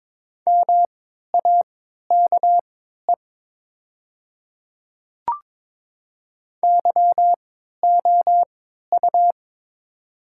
Morse code, heard as a keyed tone, about 23 wpm, spaced as for 13 wpm, 700 Hz. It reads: MAKE YOU
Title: 3x Character Spacing - Top 100 Words - 22wpm